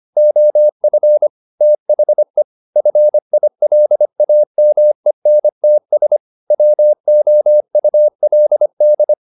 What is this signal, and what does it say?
Morse code, heard as a keyed tone, about 25 wpm, 600 Hz. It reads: OF THE FILAMENTS WOULD
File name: the old man and the sea part 1 25wpm